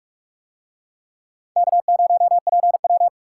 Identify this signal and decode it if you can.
Morse code, heard as a keyed tone, about 46 wpm, 700 Hz. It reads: K0PW